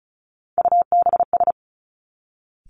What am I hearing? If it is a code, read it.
Morse code, heard as a keyed tone, about 36 wpm, 700 Hz. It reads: UBS
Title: Three Challenging Letter-Number Combinations 35wpm